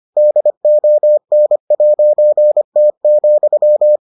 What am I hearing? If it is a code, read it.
Morse code, heard as a keyed tone, about 25 wpm, 600 Hz. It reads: DON'T,